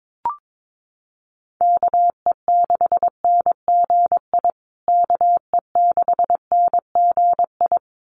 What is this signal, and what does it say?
Morse code, heard as a keyed tone, about 22 wpm, 700 Hz. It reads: KE6NGI KE6NGI
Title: USA Call Signs Repeated 2x 22wpm